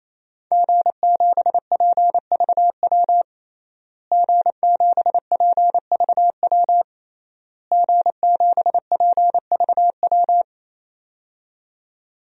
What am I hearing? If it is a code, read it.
Morse code, heard as a keyed tone, about 28 wpm, 700 Hz. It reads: G7PVW G7PVW G7PVW
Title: DXCC-100 Call Signs - 5 Characters - Repeated 3x - 28wpm